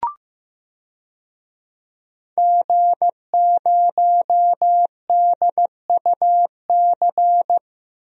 Morse code, heard as a keyed tone, about 15 wpm, 700 Hz. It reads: G0DUC
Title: DXCC-100 Call Signs - 5 Characters - 15wpm